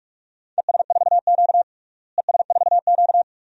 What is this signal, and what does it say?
Morse code, heard as a keyed tone, about 47 wpm, 700 Hz. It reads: ES4Q ES4Q
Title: DXCC-100 Call Signs - 4 Characters - Repeated 2x - 45wpm